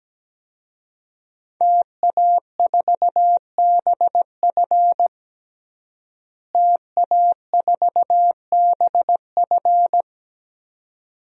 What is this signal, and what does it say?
Morse code, heard as a keyed tone, about 17 wpm, 700 Hz. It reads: TA4BF TA4BF
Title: DXCC-100 Call Signs - 5 Characters - Repeated 2x - 17wpm